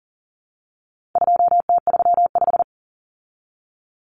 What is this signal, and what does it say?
Morse code, heard as a keyed tone, about 40 wpm, 700 Hz. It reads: 2T35
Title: Common Cut Numbers - 4-Digit 40wpm